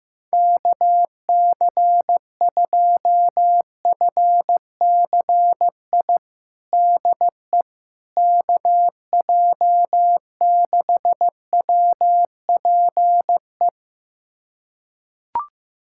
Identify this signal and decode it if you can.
Morse code, heard as a keyed tone, about 16 wpm, 700 Hz. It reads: KC2FCI DE KJ6WPE
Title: USA Callsign DE Callsign 15wpm